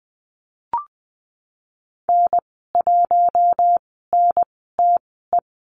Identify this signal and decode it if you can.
Morse code, heard as a keyed tone, about 20 wpm, spaced as for 15 wpm, 700 Hz. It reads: N1NTE
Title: 128 Lesson 21g - Number 6 - Call Signs 20wpm